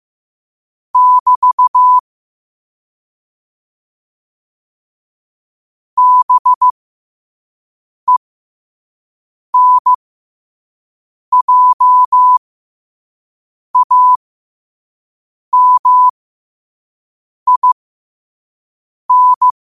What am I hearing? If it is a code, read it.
Morse code, heard as a keyed tone, about 15 wpm, spaced as for 5 wpm, 1000 Hz. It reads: <BT> BENJAMIN
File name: qotdfiles / 5wpm